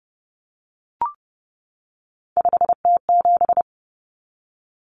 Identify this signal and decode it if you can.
Morse code, heard as a keyed tone, about 30 wpm, 700 Hz. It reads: 5T7
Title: Common Cut Numbers - 3-Digit 30wpm